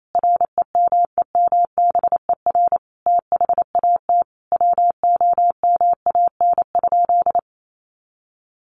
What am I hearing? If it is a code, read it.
Morse code, heard as a keyed tone, about 28 wpm, 700 Hz. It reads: REMEMBER THAT WOMAN?
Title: Sentences from Top 1100 Words 28wpm